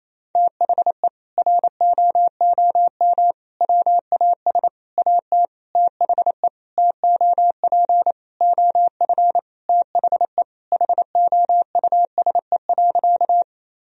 Morse code, heard as a keyed tone, about 28 wpm, 700 Hz. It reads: THE ROOM WAS AT THE TOP OF THE HOUSE.
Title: Sentences from Top 1000 Words - Review Part 2 - 28wpm